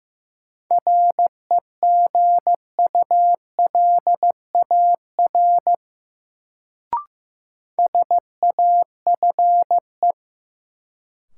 Morse code, heard as a keyed tone, about 15 wpm, 700 Hz. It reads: REGULAR SAFE
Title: Sentences from Top 900 Words - New Word Drill - 15wpm